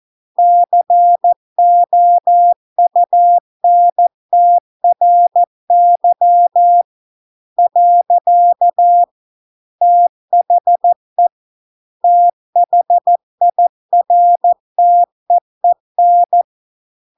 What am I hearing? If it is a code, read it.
Morse code, heard as a keyed tone, about 14 wpm, 700 Hz. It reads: COUNTRY . THE THIRTEEN